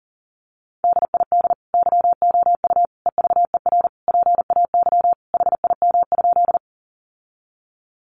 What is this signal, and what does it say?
Morse code, heard as a keyed tone, about 40 wpm, 700 Hz. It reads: DID YOU EVER PAY HIM?
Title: Sentences from Top 700 Words 40wpm